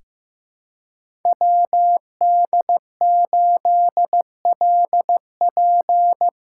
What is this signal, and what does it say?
Morse code, heard as a keyed tone, about 15 wpm, 700 Hz. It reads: WD8LP
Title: USA Call Signs - 6 Characters - 6x Character Spacing - 15wpm